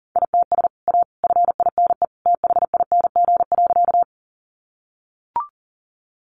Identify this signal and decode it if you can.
Morse code, heard as a keyed tone, about 40 wpm, 700 Hz. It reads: ITS A FINE THING.
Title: Sentences from Top 600 Words 40wpm